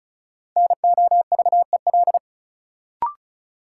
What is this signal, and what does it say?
Morse code, heard as a keyed tone, about 34 wpm, 700 Hz. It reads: NOVEL
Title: Sentences from Top 1300 Words - New Word Drill Alt - 35wpm